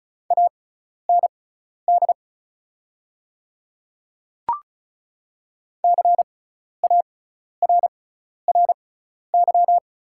Morse code, heard as a keyed tone, about 36 wpm, spaced as for 12 wpm, 700 Hz. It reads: AND CARRY
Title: 6x Character Spacing - Top 300 Words - 35wpm